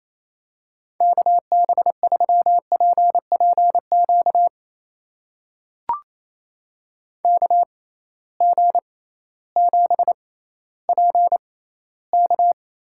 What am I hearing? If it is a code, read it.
Morse code, heard as a keyed tone, about 28 wpm, spaced as for 15 wpm, 700 Hz. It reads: KB3PPQ KG7PK
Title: USA Call Signs - 6 Characters - 6x Character Spacing - 28wpm